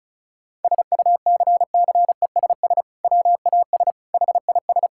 Morse code, heard as a keyed tone, about 34 wpm, 690 Hz. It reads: SUCCESS WAS HIS
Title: Sentences from Top 1000 Words - Review Part 6 - 35wpm